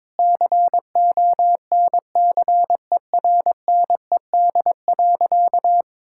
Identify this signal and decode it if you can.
Morse code, heard as a keyed tone, about 22 wpm, 700 Hz. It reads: CONCERNED.
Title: Sentences from Top 1000 Words - Review Part 6 - 22wpm